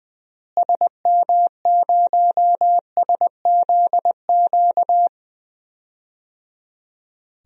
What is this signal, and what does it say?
Morse code, heard as a keyed tone, about 20 wpm, 700 Hz. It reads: SM0SZQ
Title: DXCC-100 Call Signs - 6 Characters - Repeated 3x - 20wpm